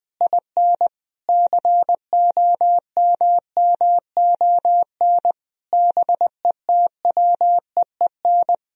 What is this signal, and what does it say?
Morse code, heard as a keyed tone, about 21 wpm, 700 Hz. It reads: IN COMMON BETWEEN
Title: Sentences from Top 1000 Words - Review Part 3 - 20wpm